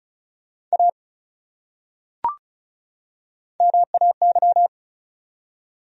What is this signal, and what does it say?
Morse code, heard as a keyed tone, about 36 wpm, 700 Hz. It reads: A MAY
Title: Top 100 Words 35wpm